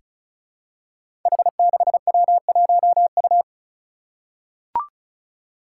Morse code, heard as a keyed tone, about 36 wpm, 700 Hz. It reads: H6W1U
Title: Five Challenging Letter-Number Combinations 35wpm